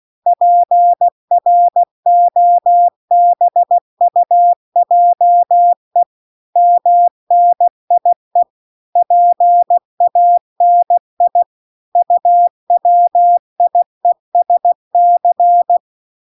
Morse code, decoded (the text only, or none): PROBUJE MNIE PANI UWIESC